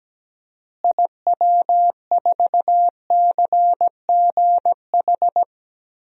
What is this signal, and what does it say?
Morse code, heard as a keyed tone, about 17 wpm, 700 Hz. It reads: IW4CGH
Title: DXCC-100 Call Signs - 6 Characters - Repeated 3x - 17wpm